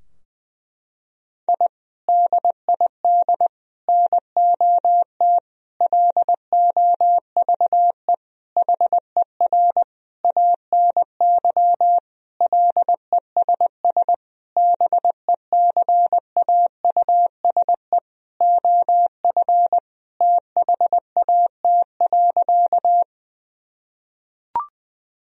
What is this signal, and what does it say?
Morse code, heard as a keyed tone, about 20 wpm, 700 Hz. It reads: I DID NOT LOVE HER ANY LESS BECAUSE OF THAT.
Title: Sentences from Top 1000 Words - Review Part 2 - 20wpm